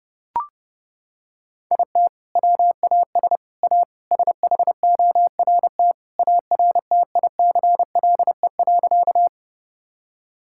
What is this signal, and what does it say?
Morse code, heard as a keyed tone, about 30 wpm, 700 Hz. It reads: IT WAS A SHORT ARTICLE.
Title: Sentences from Top 1000 Words - Review Part 5 - 30wpm